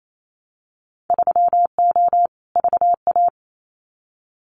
Morse code, heard as a keyed tone, about 28 wpm, 700 Hz. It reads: 3O VA
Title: Winter Field Day Exchanges 28wpm